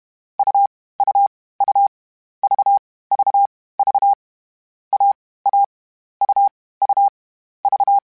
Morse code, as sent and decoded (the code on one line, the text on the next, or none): ..- / ..- / ..- / ...- / ...- / ...- / .- / .- / ..- / ..- / ...-
U U U V V V A A U U V